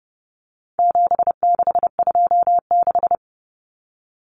Morse code, heard as a keyed tone, about 30 wpm, 700 Hz. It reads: 7626
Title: Numbers - 4-Digit 30wpm